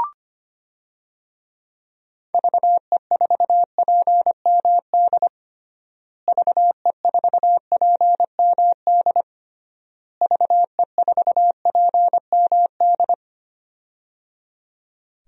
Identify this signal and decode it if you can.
Morse code, heard as a keyed tone, about 25 wpm, 700 Hz. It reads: VE4PMD VE4PMD VE4PMD
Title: DXCC-100 Call Signs - 6 Characters - Repeated 3x - 25wpm